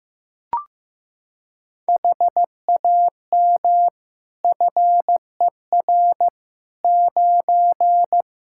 Morse code, heard as a keyed tone, about 15 wpm, 700 Hz. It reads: HAM FER 9
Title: K1USN SST QSO Practice 15wpm